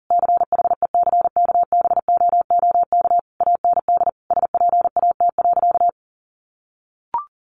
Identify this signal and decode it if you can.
Morse code, heard as a keyed tone, about 40 wpm, 700 Hz. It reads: CHECKBOOK AND SPAT.
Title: A Sound Of Thunder 40wpm